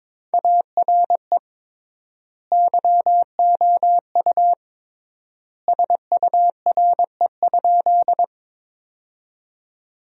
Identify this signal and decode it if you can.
Morse code, heard as a keyed tone, about 22 wpm, 700 Hz. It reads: ARE YOU SURE?
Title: 3x Word-Spacing - Sentences from Top 300 Words - 22wpm